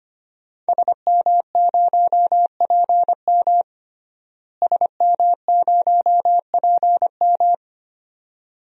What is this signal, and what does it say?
Morse code, heard as a keyed tone, about 25 wpm, 700 Hz. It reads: SM0PM SM0PM
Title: DXCC-100 Call Signs - 5 Characters - Repeated 2x - 25wpm